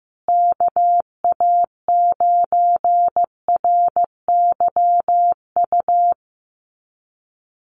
Morse code, heard as a keyed tone, about 15 wpm, 700 Hz. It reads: KA9RYU
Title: USA Call Signs Repeated 2x 15wpm